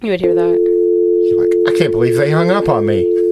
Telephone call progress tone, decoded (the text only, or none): dial tone